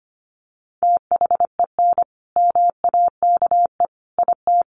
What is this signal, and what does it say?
Morse code, heard as a keyed tone, about 25 wpm, 700 Hz. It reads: THEN MAKE IT
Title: Sentences from Top 100 Words 25wpm